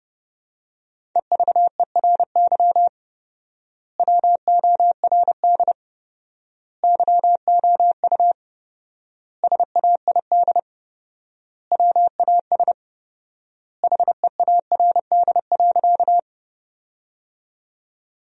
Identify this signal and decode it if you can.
Morse code, heard as a keyed tone, about 30 wpm, 700 Hz. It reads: EVERY WORD YOU SAID WAS HEARD.
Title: 4x Word-Spacing - Sentences from Top 400 Words - 30wpm